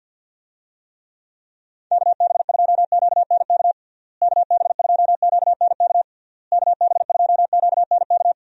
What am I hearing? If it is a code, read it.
Morse code, heard as a keyed tone, about 50 wpm, 700 Hz. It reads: KB2QNK KB2QNK KB2QNK
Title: USA Call Signs - 6 Characters - Repeated 3x - 50wpm